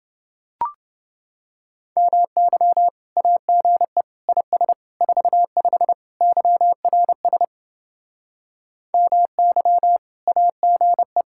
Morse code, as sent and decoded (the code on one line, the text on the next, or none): -- -.-- / .- --. / .. ... / ....- ..... / -.-- .-. ... / -- -.-- / .- --. .
MY AGE IS 45 YRS MY AGE